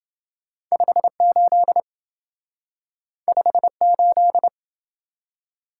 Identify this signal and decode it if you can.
Morse code, heard as a keyed tone, about 28 wpm, 700 Hz. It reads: 58 58